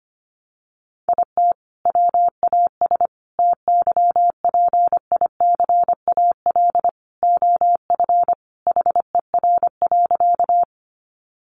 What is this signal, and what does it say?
Morse code, heard as a keyed tone, about 25 wpm, 700 Hz. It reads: IT WAS TYPICAL OF HER.
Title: Sentences from Top 1100 Words 25wpm